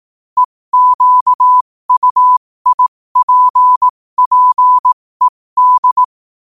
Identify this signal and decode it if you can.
Morse code, heard as a keyed tone, about 18 wpm, spaced as for 15 wpm, 1000 Hz. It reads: EQUIPPED
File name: qotdfiles / 15wpm